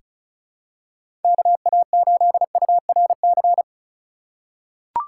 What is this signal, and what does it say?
Morse code, heard as a keyed tone, about 36 wpm, 700 Hz. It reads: KA8URC
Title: USA Call Signs - 6 Characters - 4x Character Spacing - 35wpm